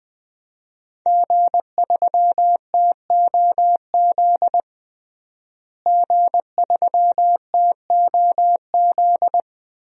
Morse code, heard as a keyed tone, about 20 wpm, 700 Hz. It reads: G3TOZ G3TOZ
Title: DXCC-100 Call Signs - 5 Characters - Repeated 3x - 20wpm